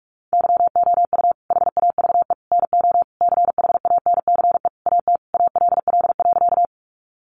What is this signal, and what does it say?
Morse code, heard as a keyed tone, about 44 wpm, 700 Hz. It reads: YOU HAVE NO CHANCE AT ALL.
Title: Sentences from Top 800 Words 45wpm